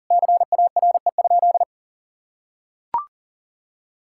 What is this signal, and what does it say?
Morse code, heard as a keyed tone, about 40 wpm, 700 Hz. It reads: CARE?
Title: Sentences from Top 1000 Words - Review Part 5 - 40wpm